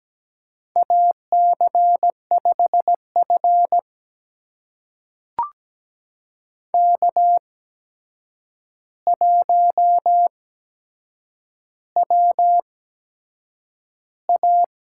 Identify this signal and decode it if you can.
Morse code, heard as a keyed tone, about 17 wpm, spaced as for 7 wpm, 700 Hz. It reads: AC5F K1WA
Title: USA Call Signs - 4 Characters - 8x Character Spacing - 17wpm